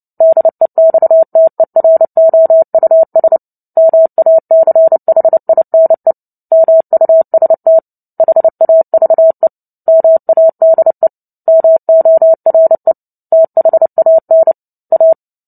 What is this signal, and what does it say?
Morse code, heard as a keyed tone, about 30 wpm, 650 Hz. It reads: DEXTEROUS MACHINE MUST HAVE MADE MORE THAN A
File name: Book Two / 03-Chapter 3 - 29 WPM